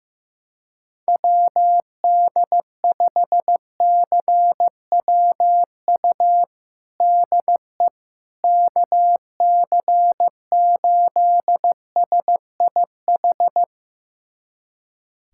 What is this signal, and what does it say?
Morse code, heard as a keyed tone, about 15 wpm, 700 Hz. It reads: WD5CWU DE KC8SIH
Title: USA Callsign DE Callsign 15wpm